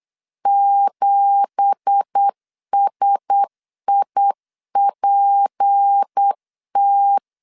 Morse code, heard as a keyed tone, about 8 wpm, 800 Hz. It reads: 7SIPT
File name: LEZIONI / 9 2867 8wpm 3440